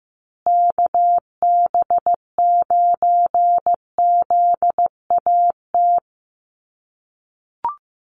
Morse code, heard as a keyed tone, about 15 wpm, 700 Hz. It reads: KB9ZAT
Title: USA Call Signs Repeated 3x 15wpm